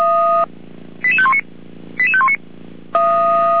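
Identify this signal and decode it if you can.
Five-tone selective call (ZVEI), heard as quick runs of five tones, tones of about 70 ms. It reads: 88519 88519
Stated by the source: ZVEI five-tone call 88517 received on an SDR, from the fire brigade server of Feuerwehr Orlamünde.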